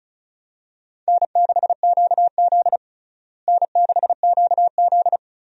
Morse code, heard as a keyed tone, about 34 wpm, 700 Hz. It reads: N6QZ N6QZ
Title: USA Call Signs - 4 Characters - Repeated 2x - 35wpm